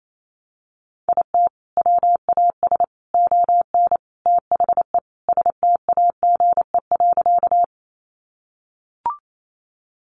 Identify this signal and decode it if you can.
Morse code, heard as a keyed tone, about 28 wpm, 700 Hz. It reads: IT WAS ON THE STAGE.
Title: Sentences from Top 1200 Words 28wpm